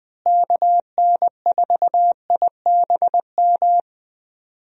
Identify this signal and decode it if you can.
Morse code, heard as a keyed tone, about 20 wpm, 700 Hz. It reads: KN4IBM
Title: USA Call Signs - 6 Characters - Repeated 2x - 20wpm